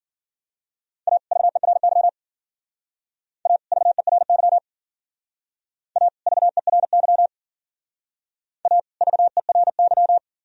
Morse code, heard as a keyed tone, about 50 wpm, 700 Hz. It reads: A VERY A VERY A VERY A VERY